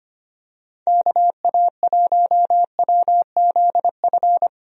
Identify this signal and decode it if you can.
Morse code, heard as a keyed tone, about 25 wpm, 700 Hz. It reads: KA1WZF
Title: USA Call Signs - 6 Characters - Repeated 3x - 25wpm